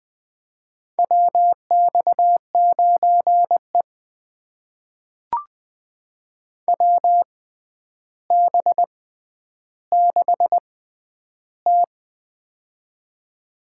Morse code, heard as a keyed tone, about 20 wpm, spaced as for 10 wpm, 700 Hz. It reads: WX9E WB6T
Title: USA Call Signs - 4 Characters - 6x Character Spacing - 20wpm